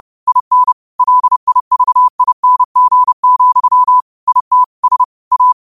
Morse code, heard as a keyed tone, about 30 wpm, 1000 Hz. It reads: IN LIVING, IT S A
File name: qotdfiles / 30wpm